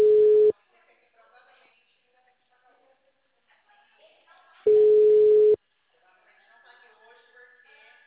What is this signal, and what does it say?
Ringback tone (Europe, 425 Hz), with speech.